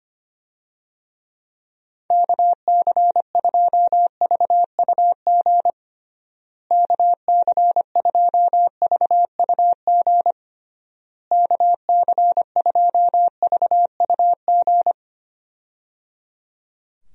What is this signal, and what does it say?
Morse code, heard as a keyed tone, about 25 wpm, 700 Hz. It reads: KC2VUG KC2VUG KC2VUG
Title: USA Call Signs - 6 Characters - Repeated 3x - 25wpm